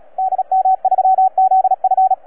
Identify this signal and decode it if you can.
Morse code, heard as a keyed tone, about 33 wpm, 750 Hz, with radio noise. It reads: DM3ZF
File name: EF8M-WAEDC-CW-2010-20-CW